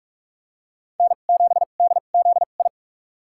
Morse code, heard as a keyed tone, about 46 wpm, spaced as for 32 wpm, 700 Hz. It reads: N7DZI
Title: USA Call Signs - 2x Character Spacing - 45wpm